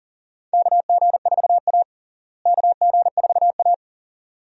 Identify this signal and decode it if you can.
Morse code, heard as a keyed tone, about 37 wpm, 700 Hz. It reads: KG4A KG4A
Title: USA Call Signs - 4 Characters - Repeated 2x - 40wpm